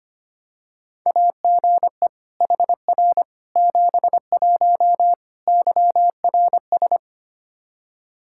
Morse code, heard as a keyed tone, about 25 wpm, 700 Hz. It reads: AGE HR 71 YRS
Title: Practice QSO - Exchange 3 - More Info - 25wpm